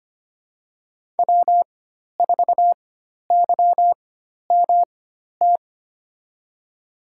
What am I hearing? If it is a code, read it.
Morse code, heard as a keyed tone, about 25 wpm, spaced as for 12 wpm, 700 Hz. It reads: W4YMT
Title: USA Call Signs - 5 Characters - 4x Character Spacing - 25wpm